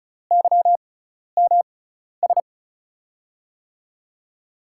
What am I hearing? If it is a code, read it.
Morse code, heard as a keyed tone, about 36 wpm, spaced as for 12 wpm, 700 Hz. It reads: YMS
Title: USA Call Signs - 6 Characters - 6x Character Spacing - 35wpm